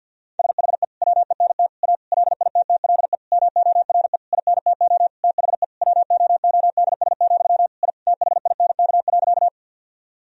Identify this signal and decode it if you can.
Morse code, heard as a keyed tone, about 47 wpm, 700 Hz. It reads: SHE WENT A LITTLE MORE INTO THE WOODS, I THINK.
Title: Sentences from Top 1000 Words - Review Part 2 - 50wpm